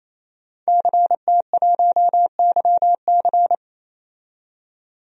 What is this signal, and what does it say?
Morse code, heard as a keyed tone, about 28 wpm, 700 Hz. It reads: CT1YC
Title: DXCC-100 Call Signs - 5 Characters - 28wpm